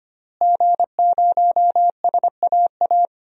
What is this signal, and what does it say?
Morse code, heard as a keyed tone, about 25 wpm, 700 Hz. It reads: G0SAA
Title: DXCC-100 Call Signs - 5 Characters - Repeated 2x - 25wpm